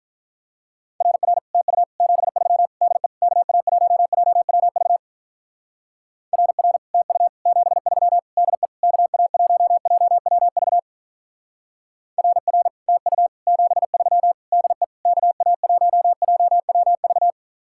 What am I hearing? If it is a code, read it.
Morse code, heard as a keyed tone, about 42 wpm, 700 Hz. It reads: RR TU 73 DE KA1JWV RR TU 73 DE KA1JWV RR TU 73 DE KA1JWV
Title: Speed-Racing Common QSO Phrases - 35wpm